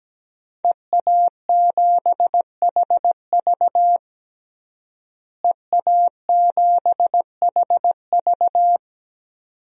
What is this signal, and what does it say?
Morse code, heard as a keyed tone, about 17 wpm, 700 Hz. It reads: EA7HV EA7HV
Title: DXCC-100 Call Signs - 5 Characters - Repeated 2x - 17wpm